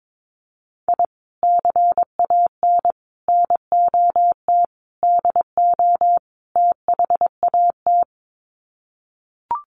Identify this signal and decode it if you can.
Morse code, heard as a keyed tone, about 22 wpm, 700 Hz. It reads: I CAN NOT DO THAT
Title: Sets of 5 Words Encore2 22wpm